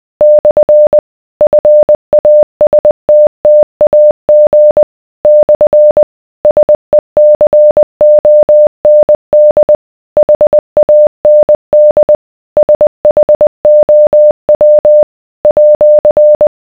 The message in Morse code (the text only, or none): / FASTTAG / SECOND HAND SHOW @